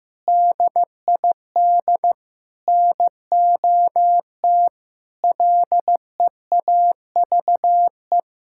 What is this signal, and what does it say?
Morse code, heard as a keyed tone, about 15 wpm, 700 Hz. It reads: DID NOT LEAVE
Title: Sentences from Top 1000 Words - Review Part 6 - 15wpm